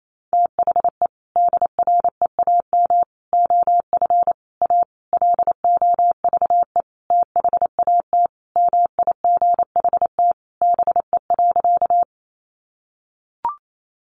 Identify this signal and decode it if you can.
Morse code, heard as a keyed tone, about 28 wpm, 700 Hz. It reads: THE DREAM OF A LOVE THAT MIGHT BE.
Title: Sentences from Top 1500 Words 28wpm